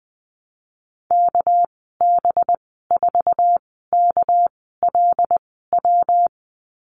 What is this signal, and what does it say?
Morse code, heard as a keyed tone, about 20 wpm, spaced as for 14 wpm, 700 Hz. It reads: KB4KLW
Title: 213 Lesson 34g - Number 8 - Call Signs 20wpm